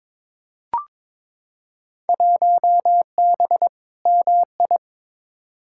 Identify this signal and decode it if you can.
Morse code, heard as a keyed tone, about 22 wpm, 700 Hz. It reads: 1B MI
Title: Field Day Exchanges - 22wpm